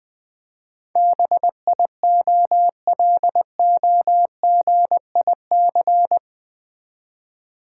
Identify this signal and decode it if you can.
Morse code, heard as a keyed tone, about 20 wpm, 700 Hz. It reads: BIOLOGIC
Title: Build A Word - 4 Letters At A Time - 10-Letter Words - 20wpm